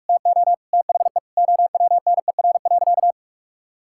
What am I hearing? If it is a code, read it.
Morse code, heard as a keyed tone, about 41 wpm, 700 Hz. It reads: TO THE OWNER.
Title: Sentences from Top 1600 Words 45wpm